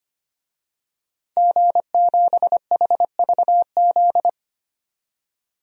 Morse code, heard as a keyed tone, about 25 wpm, 700 Hz. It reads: G7HVZ
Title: DXCC-100 Call Signs - 5 Characters - 25wpm